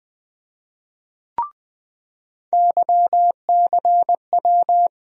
Morse code, heard as a keyed tone, about 20 wpm, 700 Hz. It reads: YCW
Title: USA Call Sign Postfixes - 3 Letters 20wpm